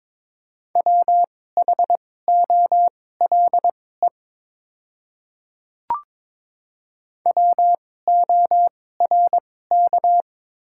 Morse code, heard as a keyed tone, about 22 wpm, spaced as for 16 wpm, 690 Hz. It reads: WHOLE WORK
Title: 2x Character Spacing - Top 500 Words - 22wpm